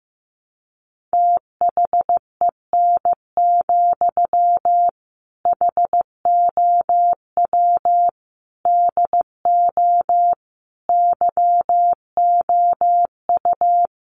Morse code, heard as a keyed tone, about 15 wpm, 700 Hz. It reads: THEN, HOW DO YOU
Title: Sentences from Top 100 Words 15wpm